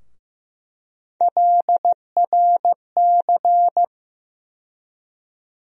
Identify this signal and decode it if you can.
Morse code, heard as a keyed tone, about 15 wpm, 700 Hz. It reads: LRC
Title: USA Call Sign Postfixes - 3 Letters 15wpm